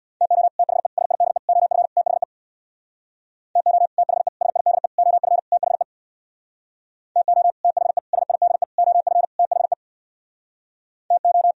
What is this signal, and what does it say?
Morse code, heard as a keyed tone, about 63 wpm, 700 Hz. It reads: TO THE SIDE OF THE TO THE SIDE OF THE TO THE SIDE OF THE TO